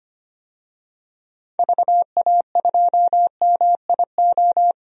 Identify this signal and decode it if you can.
Morse code, heard as a keyed tone, about 25 wpm, 700 Hz. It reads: VA2MIO
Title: DXCC-100 Call Signs - 6 Characters - 25wpm